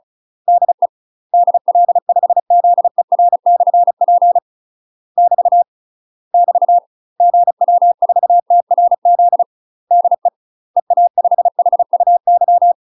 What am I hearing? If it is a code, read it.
Morse code, heard as a keyed tone, about 34 wpm, 700 Hz. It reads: DE DL5ZER/P <BT> <BT> GW4TRZ DE EA5HUY
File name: UK MASTER QSOs TRAINING / 35WPM / UK MASTER QSOs 0003